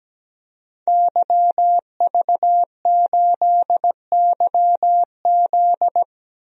Morse code, heard as a keyed tone, about 17 wpm, 700 Hz. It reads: YV8YZ
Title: DXCC-100 Call Signs - 5 Characters - 17wpm